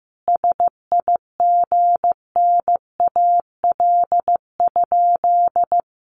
Morse code, heard as a keyed tone, about 15 wpm, 700 Hz. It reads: SIGNAL?
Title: Sentences from Top 1400 Words 15wpm